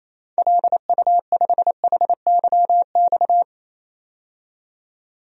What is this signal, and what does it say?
Morse code, heard as a keyed tone, about 28 wpm, 700 Hz. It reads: LU5HYX
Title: DXCC-100 Call Signs - 6 Characters - 28wpm